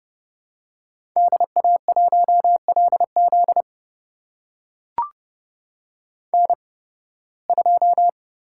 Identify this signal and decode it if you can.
Morse code, heard as a keyed tone, about 30 wpm, spaced as for 20 wpm, 700 Hz. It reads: DA1LZ N2
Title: USA Call Signs - 5 Characters - 8x Character Spacing - 30wpm